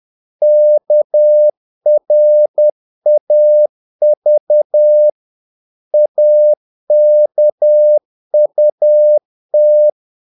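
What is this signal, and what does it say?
Morse code, heard as a keyed tone, about 10 wpm, 600 Hz. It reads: KRAV AKUT
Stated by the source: Morse practice of short Danish words at 10 wpm, from CW med Gnister.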